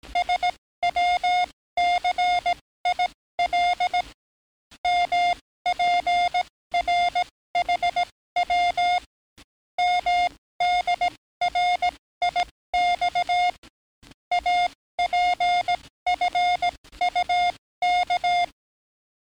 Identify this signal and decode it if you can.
Morse code, heard as a keyed tone, about 18 wpm, spaced as for 14 wpm, 710 Hz. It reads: SWCIL MPRHW MDRIX APFUK
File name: training / NAVPERS 11049to11060 / NAVPERS 11057 RA RB